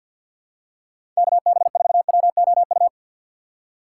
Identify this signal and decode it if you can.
Morse code, heard as a keyed tone, about 52 wpm, 700 Hz. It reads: KB4WOU
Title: USA Call Signs - 4x Character Spacing - 50wpm